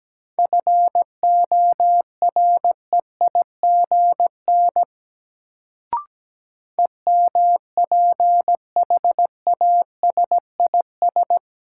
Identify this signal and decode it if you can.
Morse code, heard as a keyed tone, about 17 wpm, 700 Hz. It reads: FOREIGN EMPHASIS
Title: Sentences from Top 1400 Words - New Word Drill - 17wpm